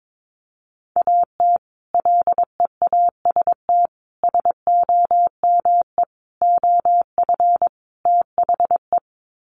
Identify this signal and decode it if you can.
Morse code, heard as a keyed tone, about 22 wpm, 700 Hz. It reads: AT LEAST SOME OF THE
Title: Sets of 5 Words Encore 22wpm